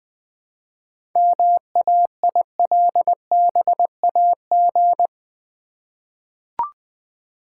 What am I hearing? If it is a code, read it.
Morse code, heard as a keyed tone, about 20 wpm, 700 Hz. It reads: MAILBAG
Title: Easy Words with Hints - 20wpm